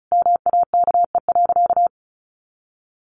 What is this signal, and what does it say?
Morse code, heard as a keyed tone, about 34 wpm, 700 Hz. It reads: MAKE.
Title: Sentences from Top 900 Words 35wpm